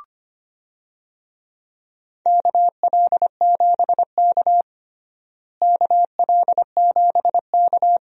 Morse code, heard as a keyed tone, about 25 wpm, 700 Hz. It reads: KL7K KL7K
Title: DXCC-100 Call Signs - 4 Characters - Repeated 3x - 25wpm